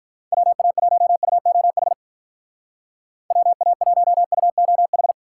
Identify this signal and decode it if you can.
Morse code, heard as a keyed tone, about 43 wpm, 700 Hz. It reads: WA1UOH WA1UOH
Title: Speed Racing - USA 2x3 Call Signs 35wpm